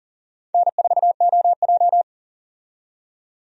Morse code, heard as a keyed tone, about 40 wpm, 700 Hz. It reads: N4OJ
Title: USA Call Signs - 4 Characters - 40wpm